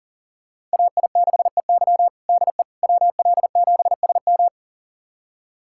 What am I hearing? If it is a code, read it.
Morse code, heard as a keyed tone, about 40 wpm, 700 Hz. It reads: AI6EY DE WL7SM
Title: USA Callsign DE Callsign - 5 Characters - 40wpm